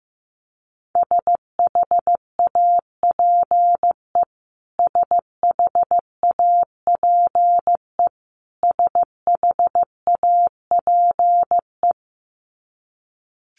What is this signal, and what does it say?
Morse code, heard as a keyed tone, about 15 wpm, 700 Hz. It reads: SHAPE SHAPE SHAPE
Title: Top 500 Words Repeated 15wpm